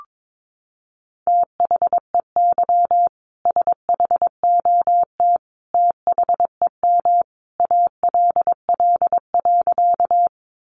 Morse code, heard as a keyed tone, about 22 wpm, 700 Hz. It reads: THEY SHOT THEM ALL.
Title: Sentences from Top 1100 Words 22wpm